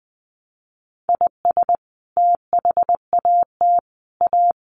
Morse code, heard as a keyed tone, about 20 wpm, 700 Hz. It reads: IS THAT A
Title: Sentences from Top 1200 Words 20wpm